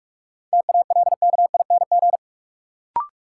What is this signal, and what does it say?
Morse code, heard as a keyed tone, about 44 wpm, 700 Hz. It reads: TALKING
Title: Sets of 2 Words - New Word Drill - 45wpm